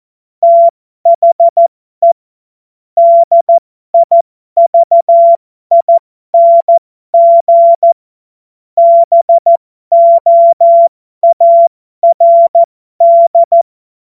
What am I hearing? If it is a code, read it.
Morse code, heard as a keyed tone, about 14 wpm, spaced as for 12 wpm, 680 Hz. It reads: THE DIVING BOARD
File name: Morsekurs / Witze